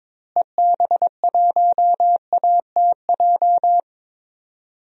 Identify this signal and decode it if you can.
Morse code, heard as a keyed tone, about 22 wpm, 700 Hz. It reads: EB1ATJ
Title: DXCC-100 Call Signs - 6 Characters - Repeated 2x - 22wpm